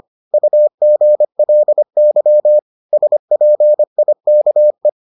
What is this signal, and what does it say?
Morse code, heard as a keyed tone, about 25 wpm, 600 Hz. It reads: UGLY SPIKE